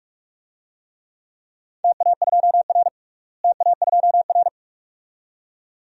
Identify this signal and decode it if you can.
Morse code, heard as a keyed tone, about 46 wpm, 700 Hz. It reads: TA2R TA2R
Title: DXCC-100 Call Signs - 4 Characters - Repeated 2x - 45wpm